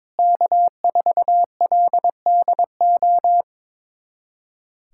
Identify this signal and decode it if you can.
Morse code, heard as a keyed tone, about 22 wpm, 700 Hz. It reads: K4LDO
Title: USA Callsign DE Callsign - 5 Characters - 22wpm